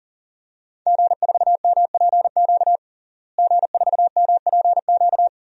Morse code, heard as a keyed tone, about 40 wpm, 700 Hz. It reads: G4MPQ G4MPQ
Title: DXCC-100 Call Signs - 5 Characters - Repeated 3x - 40wpm